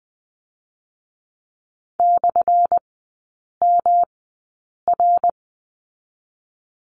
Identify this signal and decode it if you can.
Morse code, heard as a keyed tone, about 20 wpm, spaced as for 8 wpm, 700 Hz. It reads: /MR
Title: 184 Lesson 30e - Letter b - Three Characters 20wpm